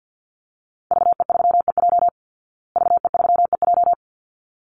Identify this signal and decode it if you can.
Morse code, heard as a keyed tone, about 50 wpm, 700 Hz. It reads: VE3EJ VE3EJ
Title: Call Signs Repeated 3x - 50wpm